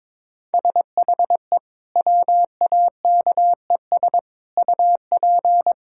Morse code, heard as a keyed tone, about 22 wpm, 700 Hz. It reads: SHE WAKES UP
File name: practice / 22wpm / Easy Sentences with Hints - 22wpm